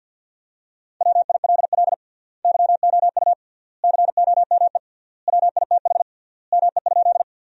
Morse code, heard as a keyed tone, about 49 wpm, 700 Hz. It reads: WILL YOU COME WITH ME?
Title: 3x Word-Spacing - Sentences from Top 200 Words - 50wpm